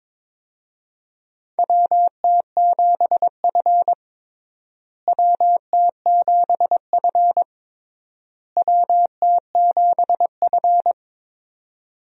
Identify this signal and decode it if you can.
Morse code, heard as a keyed tone, about 21 wpm, 700 Hz. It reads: WT7F WT7F WT7F
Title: USA Call Signs - 4 Characters - Repeated 3x - 22wpm